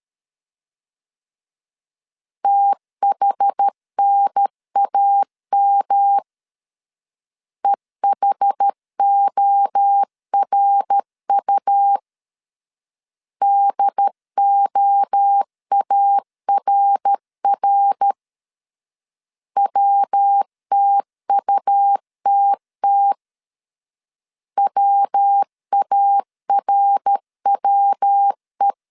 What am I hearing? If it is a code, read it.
Morse code, heard as a keyed tone, about 13 wpm, 800 Hz. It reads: THNAM EHORU DOARR WTUTT WARWE
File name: CW / DET / 1 3 1500C S C 12wpm 2750